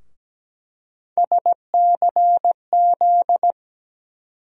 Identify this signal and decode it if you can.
Morse code, heard as a keyed tone, about 17 wpm, 700 Hz. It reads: SCZ
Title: USA Call Sign Postfixes - 3 Letters 17wpm